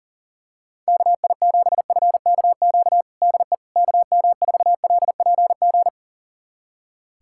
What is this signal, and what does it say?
Morse code, heard as a keyed tone, about 40 wpm, 700 Hz. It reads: KI7FKQ DE KM4LPG
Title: USA Callsign DE Callsign - 6 Characters - 40wpm